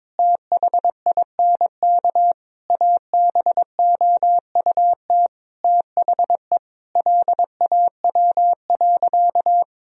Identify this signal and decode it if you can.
Morse code, heard as a keyed tone, about 22 wpm, 700 Hz. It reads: THINK ABOUT THE LAW.
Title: Sentences from Top 1000 Words - Review Part 3 - 22wpm